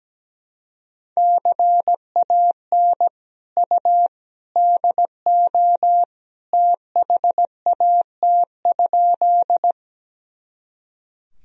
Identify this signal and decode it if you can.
Morse code, heard as a keyed tone, about 17 wpm, 700 Hz. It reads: CAN U DO THAT?
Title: Sentences from Top 100 Words and Abbreviations 17wpm